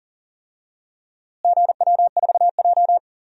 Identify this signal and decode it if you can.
Morse code, heard as a keyed tone, about 40 wpm, 700 Hz. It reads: GW4J
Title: DXCC-100 Call Signs - 4 Characters - Repeated 2x - 40wpm